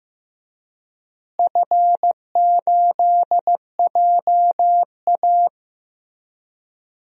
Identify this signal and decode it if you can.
Morse code, heard as a keyed tone, about 15 wpm, 700 Hz. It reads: F8JA
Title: DXCC-100 Call Signs - 4 Characters - Repeated 3x - 15wpm